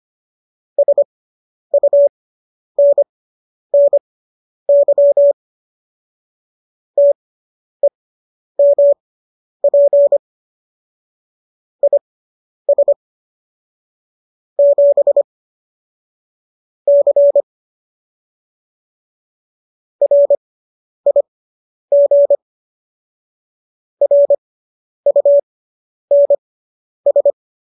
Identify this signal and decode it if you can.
Morse code, heard as a keyed tone, about 25 wpm, spaced as for 10 wpm, 570 Hz. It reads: SUNNY TEMP IS 7 C RIG RUNS
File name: qso005 10